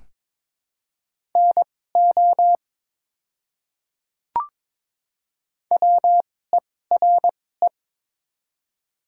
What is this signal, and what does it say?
Morse code, heard as a keyed tone, about 22 wpm, spaced as for 16 wpm, 700 Hz. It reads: NO WERE
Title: 2x Character Spacing - Top 200 Words - 22wpm